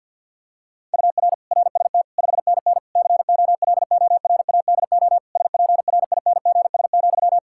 Morse code, heard as a keyed tone, about 52 wpm, 700 Hz. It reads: UR RST 5NN COLORADO SPRINGS,
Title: Practice QSO - Exchange 1 - Beginning - 50wpm